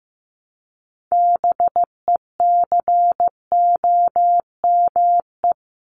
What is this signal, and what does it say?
Morse code, heard as a keyed tone, about 15 wpm, 700 Hz. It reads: BECOME
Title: Top 501-750 Words 15wpm